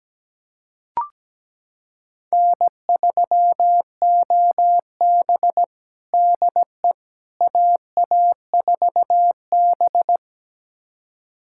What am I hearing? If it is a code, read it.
Morse code, heard as a keyed tone, about 17 wpm, 700 Hz. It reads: N3OB DE AA4B
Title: USA Callsign DE Callsign - 4 Characters - 17wpm